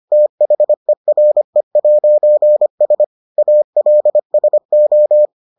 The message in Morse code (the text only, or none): THERE'S ALSO